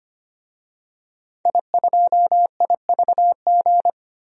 Morse code, heard as a keyed tone, about 25 wpm, 700 Hz. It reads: I2IVG
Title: DXCC-100 Call Signs - 5 Characters - 25wpm